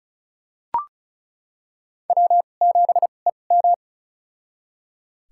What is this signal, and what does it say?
Morse code, heard as a keyed tone, about 36 wpm, spaced as for 25 wpm, 700 Hz. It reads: W7EM
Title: USA Call Signs - 2x Character Spacing - 35wpm